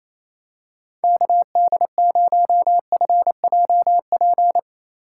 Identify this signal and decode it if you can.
Morse code, heard as a keyed tone, about 27 wpm, 700 Hz. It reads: KD0FJP
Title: USA Call Signs - 6 Characters - 2x Character Spacing - 28wpm